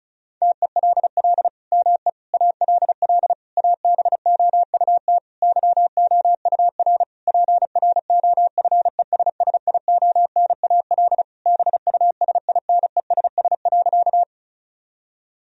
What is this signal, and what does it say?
Morse code, heard as a keyed tone, about 34 wpm, 700 Hz. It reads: TELL ME ALL ABOUT YOUR PROFESSIONAL BUSINESS.
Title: Sentences from Top 1000 Words - Review Part 4 - 35wpm